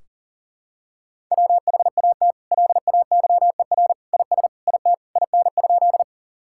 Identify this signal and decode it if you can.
Morse code, heard as a keyed tone, about 40 wpm, 700 Hz. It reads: WHAT LAYER IS IT IN?
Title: Sentences from Top 1600 Words 40wpm